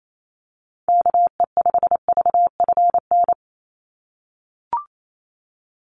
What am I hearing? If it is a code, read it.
Morse code, heard as a keyed tone, about 28 wpm, 700 Hz. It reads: KE5VFN
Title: USA Call Signs 28wpm